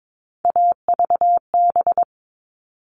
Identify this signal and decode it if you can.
Morse code, heard as a keyed tone, about 22 wpm, 700 Hz. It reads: AVB
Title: All Cut Numbers - 3-Digit 22wpm